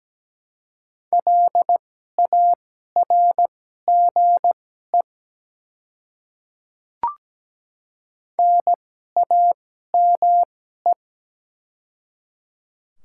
Morse code, heard as a keyed tone, about 17 wpm, spaced as for 12 wpm, 700 Hz. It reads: LARGE NAME
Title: 2x Character Spacing - Top 200 Words - 17wpm